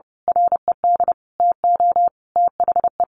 Morse code, heard as a keyed tone, about 30 wpm, 700 Hz. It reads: RED TO THE
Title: A Sound Of Thunder 30wpm